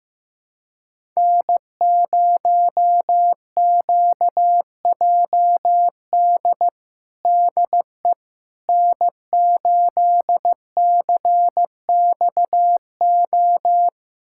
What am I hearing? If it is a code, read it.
Morse code, heard as a keyed tone, about 15 wpm, 700 Hz. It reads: N0QJD DE N8CXO
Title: USA Callsign DE Callsign - 5 Characters - 15wpm